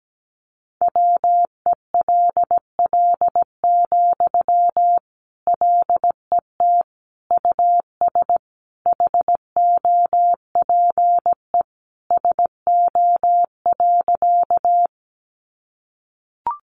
Morse code, heard as a keyed tone, about 17 wpm, 700 Hz. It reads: WELL, LET US HOPE SO.
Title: Sentences from Top 1000 Words 17wpm